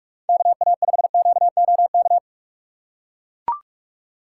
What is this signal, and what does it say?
Morse code, heard as a keyed tone, about 44 wpm, 700 Hz. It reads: KA5QOK